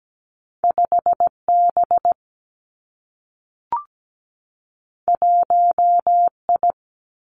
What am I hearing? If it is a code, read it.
Morse code, heard as a keyed tone, about 17 wpm, 700 Hz. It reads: 5 B 1 I
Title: Two Challenging Letter-Number Combinations 17wpm